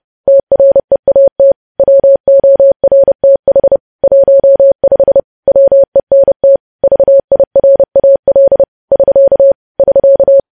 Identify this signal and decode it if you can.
Morse code, heard as a keyed tone, about 30 wpm, 550 Hz. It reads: TREAT WORTH 15 WENT VIRAL <SK> <SK>
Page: news headlines in morse code at 30 wpm